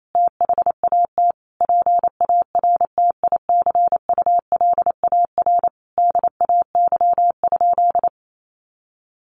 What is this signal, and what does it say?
Morse code, heard as a keyed tone, about 28 wpm, 700 Hz. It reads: THAT PARTICULAR DAY?
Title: Sentences from Top 1400 Words 28wpm